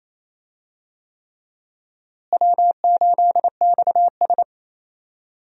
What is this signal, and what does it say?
Morse code, heard as a keyed tone, about 28 wpm, 700 Hz. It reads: W8XS
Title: USA Call Signs - 4 Characters - Repeated 3x - 28wpm